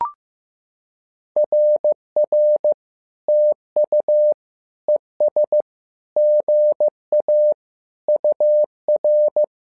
Morse code, heard as a keyed tone, about 15 wpm, 600 Hz. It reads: RR TU ES GA UR
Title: SOTA QSO Practice 15wpm